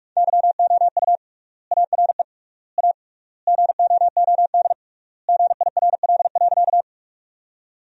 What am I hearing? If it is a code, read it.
Morse code, heard as a keyed tone, about 48 wpm, 700 Hz. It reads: YOU ARE A GOOD GIRL.
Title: 3x Word-Spacing - Sentences from Top 400 Words - 45wpm